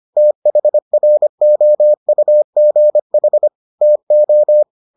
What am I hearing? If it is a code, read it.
Morse code, heard as a keyed tone, about 25 wpm, 600 Hz. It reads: THROUGH TO